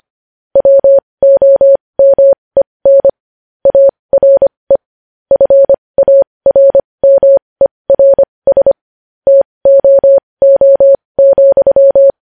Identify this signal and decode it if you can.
Morse code, heard as a keyed tone, about 25 wpm, spaced as for 20 wpm, 550 Hz. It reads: WOMEN ARE FARMERS TOO,
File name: podcast / News headlines-1x-25x20